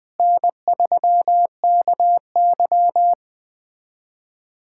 Morse code, heard as a keyed tone, about 20 wpm, 700 Hz. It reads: N3KY
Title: USA Callsign DE Callsign - 4 Characters - 20wpm